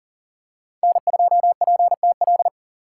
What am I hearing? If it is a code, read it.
Morse code, heard as a keyed tone, about 40 wpm, 700 Hz. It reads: N2PTL
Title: USA Call Signs - 4x Character Spacing - 40wpm